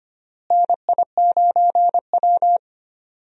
Morse code, heard as a keyed tone, about 25 wpm, 700 Hz. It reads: NI9W
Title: USA Call Signs - 4 Characters - Repeated 3x - 25wpm